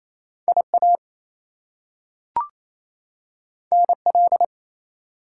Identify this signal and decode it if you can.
Morse code, heard as a keyed tone, about 28 wpm, 700 Hz. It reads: IA NL
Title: US States and Canadian Provinces 28wpm